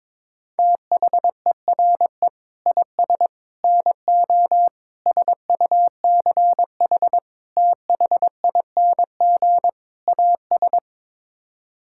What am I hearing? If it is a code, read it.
Morse code, heard as a keyed tone, about 22 wpm, 700 Hz. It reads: THERE IS NO SUCH THING AS
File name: practice / 22wpm / Sets of 6 Words - 22wpm